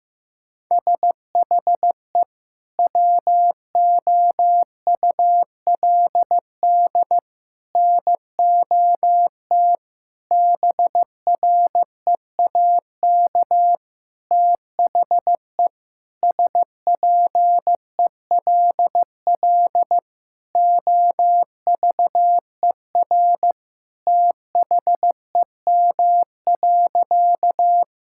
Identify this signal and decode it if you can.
Morse code, heard as a keyed tone, about 16 wpm, 700 Hz. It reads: SHE WOULD NOT BREAK THE SPELL OVER THEM.
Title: Sentences from Top 1000 Words - Review Part 6 - 15wpm